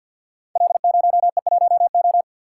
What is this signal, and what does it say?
Morse code, heard as a keyed tone, about 51 wpm, 700 Hz. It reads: L0E1O
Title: Code Groups - 5 Letter:Numbers - 50wpm